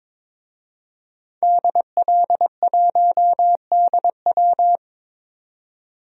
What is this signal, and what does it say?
Morse code, heard as a keyed tone, about 22 wpm, 700 Hz. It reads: DL1DW
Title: DXCC-100 Call Signs - 5 Characters - 22wpm